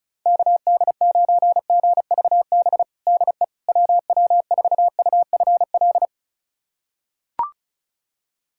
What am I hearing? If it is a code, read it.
Morse code, heard as a keyed tone, about 36 wpm, 700 Hz. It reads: KD9GVB DE WW4UFL
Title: USA Callsign DE Callsign - 6 Characters - 35wpm